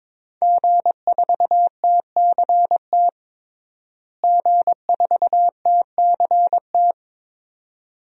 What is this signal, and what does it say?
Morse code, heard as a keyed tone, about 22 wpm, 700 Hz. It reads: G4TCT G4TCT
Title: DXCC-100 Call Signs - 5 Characters - Repeated 2x - 22wpm